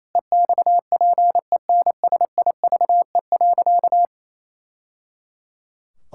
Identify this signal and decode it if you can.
Morse code, heard as a keyed tone, about 28 wpm, 700 Hz. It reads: EXPENSIVE.
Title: Sentences from Top 1000 Words - Review Part 6 - 28wpm